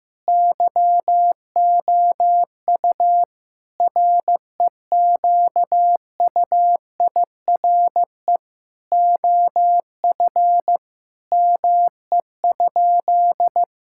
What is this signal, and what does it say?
Morse code, heard as a keyed tone, about 15 wpm, 700 Hz. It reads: YOU REQUIRE OF ME?
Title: Sentences from Top 1000 Words - Review Part 6 - 15wpm